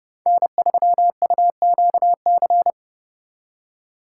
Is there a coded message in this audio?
Yes: Morse code from 0.3 to 2.7 s, heard as a keyed tone, about 39 wpm, 700 Hz. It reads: N3UQC